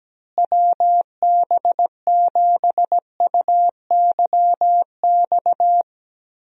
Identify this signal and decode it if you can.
Morse code, heard as a keyed tone, about 17 wpm, 700 Hz. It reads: WB7UYX
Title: USA Call Signs - 6 Characters - Repeated 3x - 17wpm